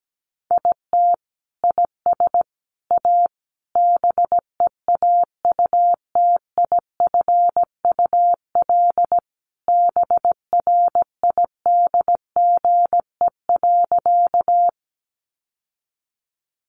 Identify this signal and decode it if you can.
Morse code, heard as a keyed tone, about 17 wpm, 700 Hz. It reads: IT IS A BEAUTIFUL BRIDGE.
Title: Sentences from Top 1200 Words 17wpm